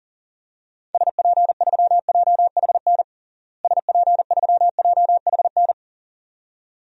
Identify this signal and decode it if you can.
Morse code, heard as a keyed tone, about 37 wpm, 700 Hz. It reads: SP3JHN SP3JHN
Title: DXCC-100 Call Signs - 6 Characters - Repeated 2x - 40wpm